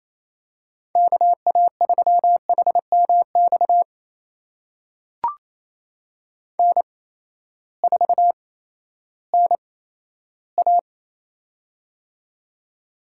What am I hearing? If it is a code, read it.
Morse code, heard as a keyed tone, about 28 wpm, spaced as for 14 wpm, 700 Hz. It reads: KA3HMX N4NA
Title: USA Call Signs - 8x Character Spacing - 28wpm